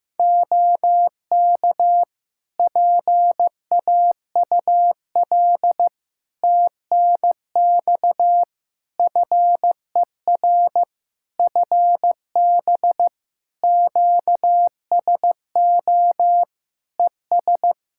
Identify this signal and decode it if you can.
Morse code, heard as a keyed tone, about 15 wpm, 700 Hz. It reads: OK PAUL TNX FER FB QSO ES
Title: K1USN SST QSO Practice 15wpm